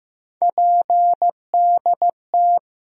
Morse code, heard as a keyed tone, about 15 wpm, 700 Hz. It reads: PDT
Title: USA Call Sign Postfixes - 3 Letters 15wpm